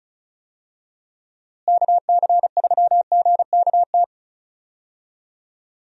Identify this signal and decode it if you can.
Morse code, heard as a keyed tone, about 34 wpm, 700 Hz. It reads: KC3GKT
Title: USA Call Signs - 6 Characters - Repeated 3x - 35wpm